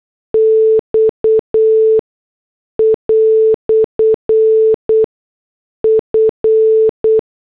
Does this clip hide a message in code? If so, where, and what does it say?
Yes: Morse code from 0.3 to 7.2 s, heard as a keyed tone, about 8 wpm, spaced as for 6 wpm, 440 Hz. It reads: X" F